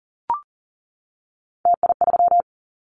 Morse code, heard as a keyed tone, about 40 wpm, 700 Hz. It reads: TI3
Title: Callsign Prefixes for Top 100 DXCC - 40wpm